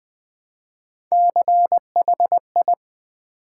Morse code, heard as a keyed tone, about 20 wpm, 700 Hz. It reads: CHI
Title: Build A Word - 1 Letter At A Time - 3 to 5-Letter Words - 20wpm